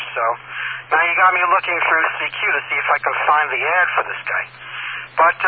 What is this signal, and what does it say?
Emergency Alert System (SAME) tones (header bursts), heard as harsh buzzing bursts, with speech. It is heard from 0.4 to 4.7 s.